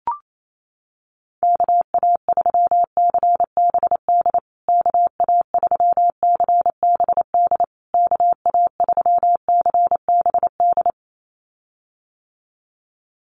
Morse code, heard as a keyed tone, about 28 wpm, 700 Hz. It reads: KA3CBD KA3CBD KA3CBD
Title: USA Call Signs Repeated 3x 28wpm